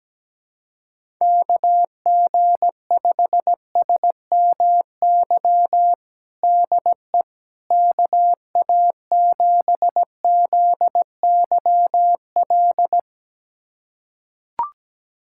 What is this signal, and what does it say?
Morse code, heard as a keyed tone, about 17 wpm, 700 Hz. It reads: KG5SMY DE KA7ZYL
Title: USA Callsign DE Callsign - 6 Characters - 17wpm